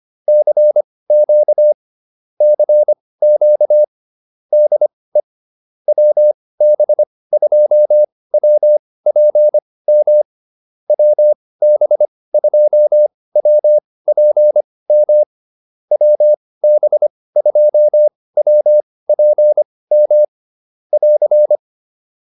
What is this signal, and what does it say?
Morse code, heard as a keyed tone, about 25 wpm, spaced as for 18 wpm, 600 Hz. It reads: CQ CQ DE WB2WPM WB2WPM WB2WPM <AR>
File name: QSO106 18